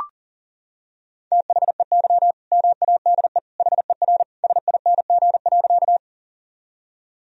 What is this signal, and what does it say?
Morse code, heard as a keyed tone, about 40 wpm, 700 Hz. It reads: THEY MADE HER SING.
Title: Sentences from Top 1000 Words - Review Part 2 - 40wpm